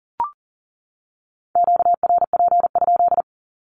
Morse code, heard as a keyed tone, about 40 wpm, 700 Hz. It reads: QRP?
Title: Q-Codes 40wpm